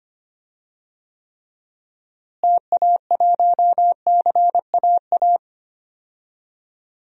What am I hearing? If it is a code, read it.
Morse code, heard as a keyed tone, about 25 wpm, 700 Hz. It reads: TA1CAA
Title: DXCC-100 Call Signs - 6 Characters - Repeated 2x - 25wpm